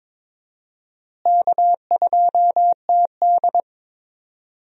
Morse code, heard as a keyed tone, about 22 wpm, 700 Hz. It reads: K2TD
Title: USA Call Signs - 4 Characters - Repeated 2x - 22wpm